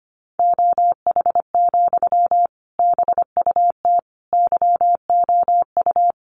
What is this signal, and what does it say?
Morse code, heard as a keyed tone, about 25 wpm, 700 Hz. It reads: OH, BUT YOU
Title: Sentences from Top 500 Words 25wpm